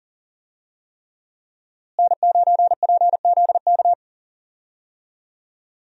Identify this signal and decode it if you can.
Morse code, heard as a keyed tone, about 37 wpm, 700 Hz. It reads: N9PZK
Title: USA Call Signs - 5 Characters - Repeated 3x - 40wpm